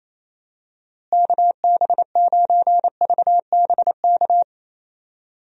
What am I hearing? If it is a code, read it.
Morse code, heard as a keyed tone, about 28 wpm, 700 Hz. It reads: KB9VBK
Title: USA Call Signs - 6 Characters - Repeated 2x - 28wpm